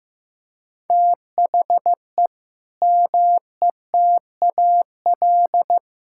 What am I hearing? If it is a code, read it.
Morse code, heard as a keyed tone, about 15 wpm, 700 Hz. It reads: THE METAL
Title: Sentences from Top 1000 Words - Review Part 6 - 15wpm